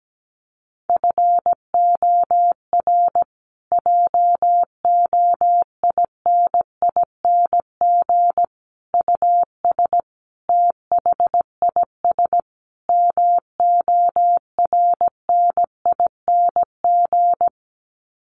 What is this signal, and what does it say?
Morse code, heard as a keyed tone, about 17 wpm, 700 Hz. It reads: FOR JOINING US THIS MORNING
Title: Sets of 5 Words Encore 17wpm